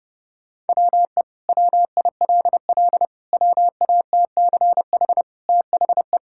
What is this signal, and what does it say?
Morse code, heard as a keyed tone, about 30 wpm, 700 Hz. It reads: WE WILL WATCH THE
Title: Sentences from Top 1000 Words - Review Part 5 - 30wpm